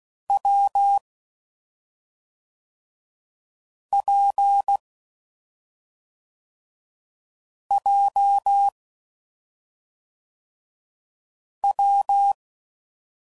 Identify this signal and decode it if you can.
Morse code, heard as a keyed tone, about 16 wpm, 800 Hz. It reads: W P J W
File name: Combinations / Difficult Combos / WPJ WPJ1